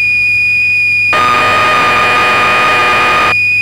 Cassette data (Kansas City type).
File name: magazines / Acorn User / 1984-11 BBC Programs